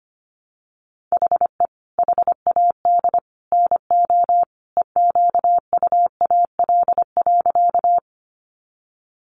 Morse code, heard as a keyed tone, about 25 wpm, 700 Hz. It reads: HE HAD NO EQUAL.
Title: Sentences from Top 1400 Words 25wpm